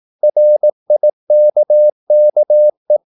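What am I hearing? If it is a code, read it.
Morse code, heard as a keyed tone, about 18 wpm, 600 Hz. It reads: RIKKE
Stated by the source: Slow (QRS) Morse practice at 18 wpm, CW med Gnister.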